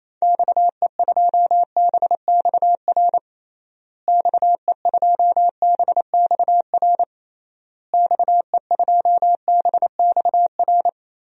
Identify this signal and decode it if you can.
Morse code, heard as a keyed tone, about 28 wpm, 700 Hz. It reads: XE2BXR XE2BXR XE2BXR
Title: DXCC-100 Call Signs - 6 Characters - Repeated 3x - 28wpm